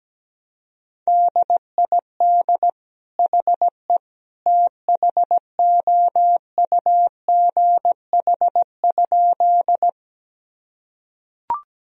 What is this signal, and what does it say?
Morse code, heard as a keyed tone, about 17 wpm, 700 Hz. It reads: DID HE THOUGH?
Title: Sentences from Top 1000 Words - Review Part 2 - 17wpm